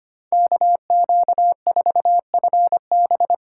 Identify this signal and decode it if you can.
Morse code, heard as a keyed tone, about 25 wpm, 700 Hz. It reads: KQ4FB
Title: USA Call Signs - 5 Characters - 4x Character Spacing - 25wpm